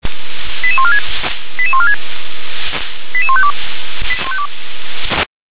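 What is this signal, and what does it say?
Five-tone selective call (ZVEI), heard as quick runs of five tones, tones of about 70 ms. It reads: 88146 88146 88152
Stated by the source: ZVEI five-tone selective call 88702 received by SDR.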